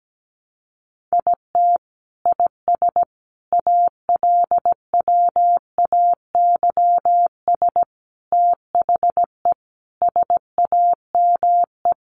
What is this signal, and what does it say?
Morse code, heard as a keyed tone, about 17 wpm, 700 Hz. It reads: IT IS ALWAYS THE SAME
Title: Sentences from Top 1400 Words 17wpm